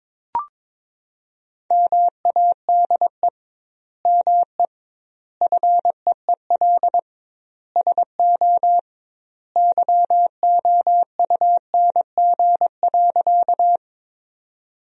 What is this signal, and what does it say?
Morse code, heard as a keyed tone, about 22 wpm, 700 Hz. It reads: MADE ME FEEL SO YOUNG.
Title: 2x Word-Spacing - Sentences from Top 400 Words - 22wpm